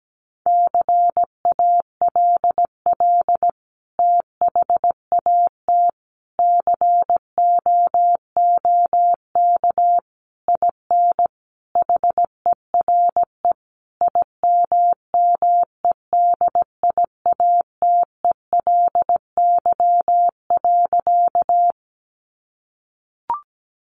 Morse code, heard as a keyed tone, about 17 wpm, 700 Hz. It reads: CALL THAT COOK IN HERE IMMEDIATELY.
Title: Sentences from Top 1100 Words 17wpm